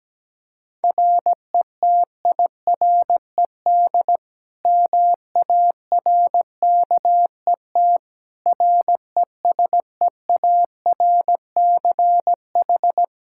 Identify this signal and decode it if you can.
Morse code, heard as a keyed tone, about 17 wpm, 700 Hz. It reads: RETIRED MARKET RESEARCH
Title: Practice QSO - Exchange 3 - More Info - 17wpm